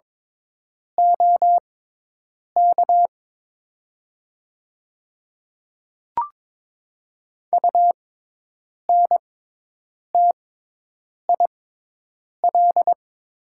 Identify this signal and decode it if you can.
Morse code, heard as a keyed tone, about 22 wpm, spaced as for 8 wpm, 700 Hz. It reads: OK UNTIL
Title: 6x Character Spacing - Top 500 Words - 22wpm